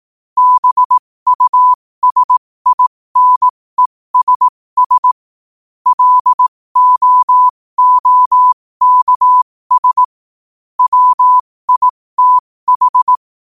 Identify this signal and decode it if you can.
Morse code, heard as a keyed tone, about 18 wpm, spaced as for 15 wpm, 1000 Hz. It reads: BUSINESS LOOKS WITH
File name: qotdfiles / 15wpm